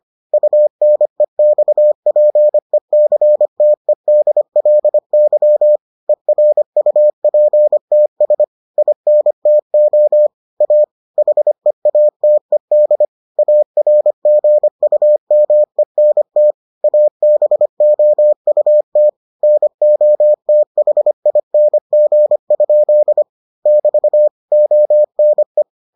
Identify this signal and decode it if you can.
Morse code, heard as a keyed tone, about 25 wpm, 600 Hz. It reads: UNEXPECTEDLY ERUPTS INTO A HEATED ARGUMENT ABOUT NOTHING? <BT> ONE